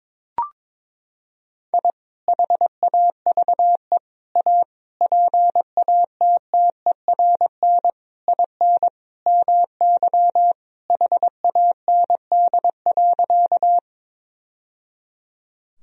Morse code, heard as a keyed tone, about 21 wpm, 700 Hz. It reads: I HAVE A PATTERN IN MY HAND.
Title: Sentences from Top 1000 Words - Review Part 2 - 22wpm